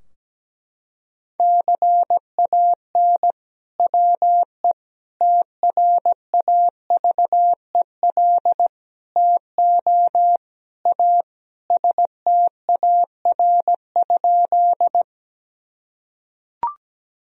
Morse code, heard as a keyed tone, about 17 wpm, 700 Hz. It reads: CAN WE TRAVEL TO A STAR?
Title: Sentences from Top 1000 Words - Review Part 2 - 17wpm